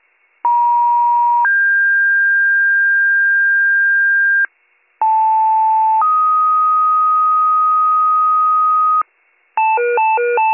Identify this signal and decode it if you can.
Two-tone page, heard as a one-second tone, then a three-second tone, about 950 then 1600 Hz, about 880 then 1250 Hz, with noise. It is heard from 0.4 to 9.0 s.